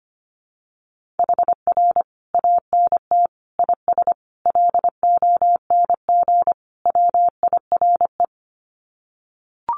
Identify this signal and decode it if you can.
Morse code, heard as a keyed tone, about 25 wpm, 700 Hz. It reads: HR ANT IS LONG WIRE
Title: Common QSO Phrases 25wpm